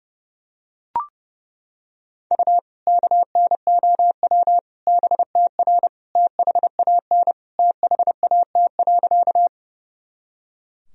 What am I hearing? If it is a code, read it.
Morse code, heard as a keyed tone, about 30 wpm, 700 Hz. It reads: U KNOW BTR THAN THAT.
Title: Sentences from Top 400 Words and Abbreviations 30wpm